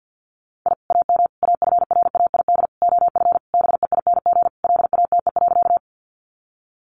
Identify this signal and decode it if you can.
Morse code, heard as a keyed tone, about 50 wpm, 700 Hz. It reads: I AM AFRAID OF BEING LATE.
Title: Sentences from Top 1500 Words 50wpm